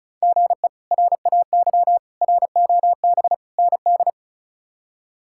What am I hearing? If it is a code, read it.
Morse code, heard as a keyed tone, about 34 wpm, 700 Hz. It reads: GE RAY ROB ND
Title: K1USN SST QSO Practice - 35wpm